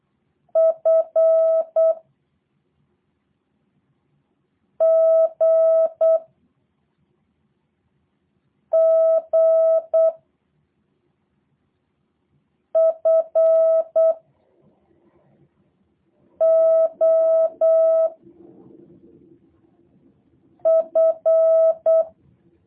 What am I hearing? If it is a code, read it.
Morse code, heard as a keyed tone, about 8 wpm, 650 Hz. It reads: F G G F O F
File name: jh8cbh / 2023cw / review1-4